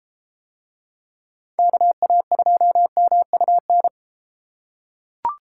Morse code, heard as a keyed tone, about 31 wpm, 700 Hz. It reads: KA2MUN